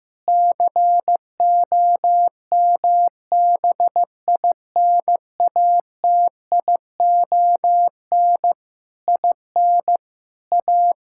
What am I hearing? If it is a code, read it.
Morse code, heard as a keyed tone, about 16 wpm, 700 Hz. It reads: COMBINATION IN A
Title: Sentences from Top 1000 Words - Review Part 6 - 15wpm